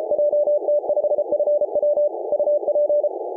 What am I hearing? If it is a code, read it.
Morse code, heard as a keyed tone, about 34 wpm, 590 Hz. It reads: JT5FW UP
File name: mongolia on 18mhz cw